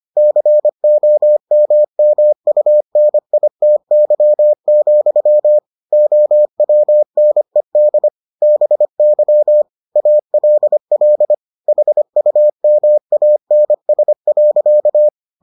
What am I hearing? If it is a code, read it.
Morse code, heard as a keyed tone, about 25 wpm, 600 Hz. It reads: COMMUNITY, OWNED BY ALL HUMANS.